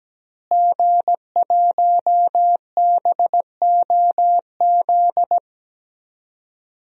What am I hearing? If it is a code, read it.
Morse code, heard as a keyed tone, about 17 wpm, 700 Hz. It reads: G1BOZ
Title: DXCC-100 Call Signs - 5 Characters - Repeated 3x - 17wpm